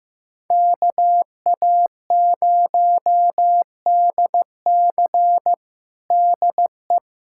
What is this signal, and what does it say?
Morse code, heard as a keyed tone, about 15 wpm, 700 Hz. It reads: KA0DC DE
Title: Practice QSO - Exchange 4 - Ending - 15wpm